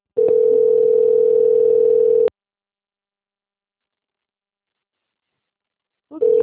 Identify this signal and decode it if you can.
Ringback tone (North America), 440 + 480 Hz.
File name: out- - - - - .25056